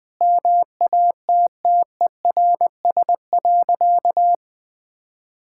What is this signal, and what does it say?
Morse code, heard as a keyed tone, about 20 wpm, 700 Hz. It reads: MATTERS.
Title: Sentences with 9-Word Phrases - Spoken Beginning - 20wpm